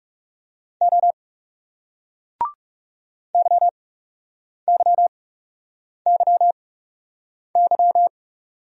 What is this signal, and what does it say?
Morse code, heard as a keyed tone, about 42 wpm, spaced as for 8 wpm, 700 Hz. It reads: O YYYY